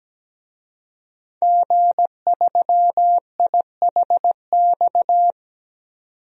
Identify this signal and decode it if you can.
Morse code, heard as a keyed tone, about 17 wpm, 700 Hz. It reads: G3IHX
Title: DXCC-100 Call Signs - 5 Characters - 17wpm